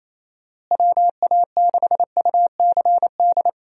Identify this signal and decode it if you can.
Morse code, heard as a keyed tone, about 28 wpm, 700 Hz. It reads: WA6UCD
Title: USA Call Signs - 6 Characters - Repeated 3x - 28wpm